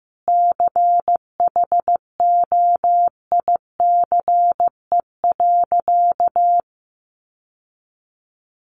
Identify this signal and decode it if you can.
Morse code, heard as a keyed tone, about 15 wpm, 700 Hz. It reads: CHOICE.
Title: Sentences from Top 900 Words 15wpm